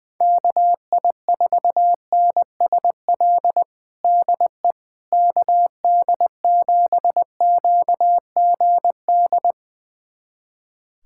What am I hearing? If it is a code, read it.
Morse code, heard as a keyed tone, about 21 wpm, 700 Hz. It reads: KI4NSL DE KD7QGD
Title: USA Callsign DE Callsign - 6 Characters - 20wpm